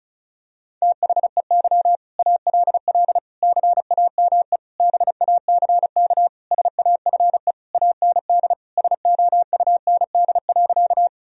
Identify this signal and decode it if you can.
Morse code, heard as a keyed tone, about 34 wpm, 700 Hz. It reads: THEY ALL CAME BACK SAFE AND SOUND.
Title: Sentences from Top 1000 Words - Review Part 1 - 35wpm